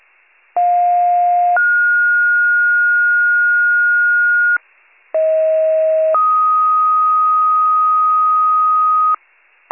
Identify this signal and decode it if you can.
Two-tone page, heard as a one-second tone, then a three-second tone, about 710 then 1430 Hz, about 640 then 1250 Hz, with noise. It is heard from 0.6 to 9.2 s.